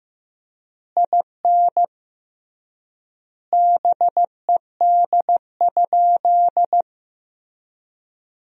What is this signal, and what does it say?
Morse code, heard as a keyed tone, about 15 wpm, 700 Hz. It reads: IN BED?
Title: 3x Word-Spacing - Sentences from Top 500 Words - 15wpm